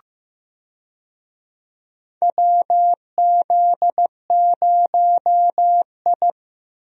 Morse code, heard as a keyed tone, about 15 wpm, 700 Hz. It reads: WZ0I
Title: USA Call Signs - 4 Characters - Repeated 3x - 15wpm